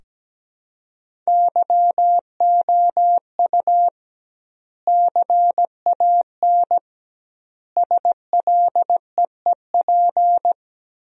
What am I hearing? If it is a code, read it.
Morse code, heard as a keyed tone, about 17 wpm, 700 Hz. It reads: YOU CAN SLEEP
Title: 2x Word-Spacing - Sentences from Top 500 Words - 17wpm